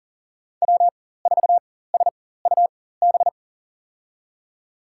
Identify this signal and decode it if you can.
Morse code, heard as a keyed tone, about 40 wpm, spaced as for 19 wpm, 700 Hz. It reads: W4SUB
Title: USA Call Signs - 4x Character Spacing - 40wpm